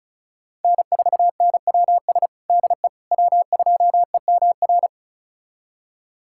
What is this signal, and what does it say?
Morse code, heard as a keyed tone, about 36 wpm, 700 Hz. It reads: N4NWS DE W2EMR
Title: USA Callsign DE Callsign - 5 Characters - 35wpm